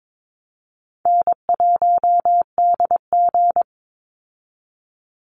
Morse code, heard as a keyed tone, about 22 wpm, 700 Hz. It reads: N1DG
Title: Easy Call Signs Repeated 3x - 22wpm